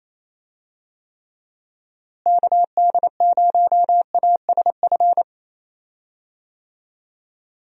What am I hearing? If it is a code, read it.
Morse code, heard as a keyed tone, about 28 wpm, 700 Hz. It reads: KD0ASF
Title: USA Call Signs - 6 Characters - Repeated 3x - 28wpm